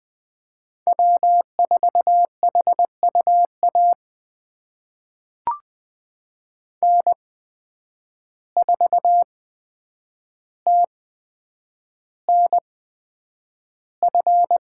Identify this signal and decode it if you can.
Morse code, heard as a keyed tone, about 20 wpm, spaced as for 9 wpm, 700 Hz. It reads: W4HUA N4TNF
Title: USA Call Signs - 5 Characters - 8x Character Spacing - 20wpm